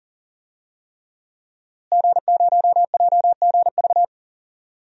Morse code, heard as a keyed tone, about 40 wpm, 700 Hz. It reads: G0JGV
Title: DXCC-100 Call Signs - 5 Characters - Repeated 3x - 40wpm